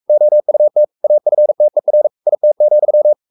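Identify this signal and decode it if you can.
Morse code, heard as a keyed tone, about 44 wpm, 600 Hz. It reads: OUT AFTER IT,
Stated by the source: Morse code at 43 wpm sending Ken Follett's novel Never in English, from CW med Gnister.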